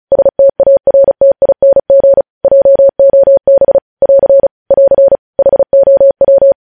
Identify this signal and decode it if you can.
Morse code, heard as a keyed tone, about 36 wpm, 550 Hz. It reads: STARTING JOB <AR> <AR> HOW